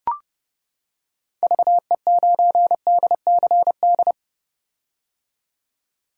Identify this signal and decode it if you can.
Morse code, heard as a keyed tone, about 29 wpm, 700 Hz. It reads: VE9DCD
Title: K1USN SST QSO Practice - 30wpm